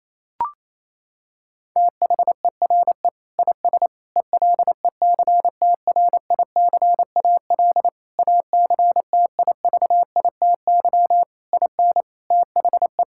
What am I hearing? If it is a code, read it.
Morse code, heard as a keyed tone, about 28 wpm, 700 Hz. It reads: THERE IS ELECTRICAL ACTIVITY IN THE
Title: Sentences from Top 1000 Words - Review Part 6 - 28wpm